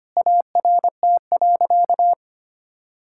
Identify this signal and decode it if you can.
Morse code, heard as a keyed tone, about 25 wpm, 700 Hz. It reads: ART.
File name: practice / 25wpm / Sentences with 4-Word Phrases - 25wpm